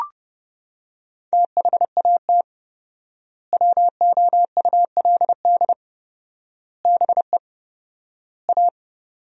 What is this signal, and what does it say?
Morse code, heard as a keyed tone, about 30 wpm, 700 Hz. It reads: THAT WOULD BE A
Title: 4x Word-Spacing - Sentences from Top 500 Words - 30wpm